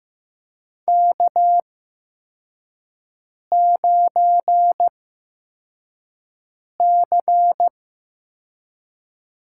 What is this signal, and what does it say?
Morse code, heard as a keyed tone, about 15 wpm, spaced as for 4 wpm, 700 Hz. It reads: K9C
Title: USA Call Signs - 8x Character Spacing - 15wpm